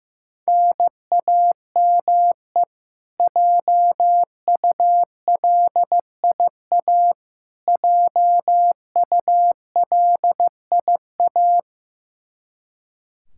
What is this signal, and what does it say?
Morse code, heard as a keyed tone, about 15 wpm, 700 Hz. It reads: NAME JULIA JULIA
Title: Practice QSO - Exchange 1 - Beginning - 15wpm